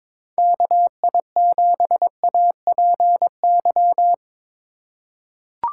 Morse code, heard as a keyed tone, about 22 wpm, 700 Hz. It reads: KI7APY
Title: USA Call Signs - 6 Characters - 3x Character Spacing - 22wpm